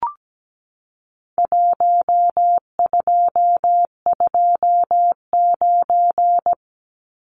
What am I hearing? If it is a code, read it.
Morse code, heard as a keyed tone, about 17 wpm, 700 Hz. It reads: 1229
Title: Numbers - Contest Style - 4-Digit 17wpm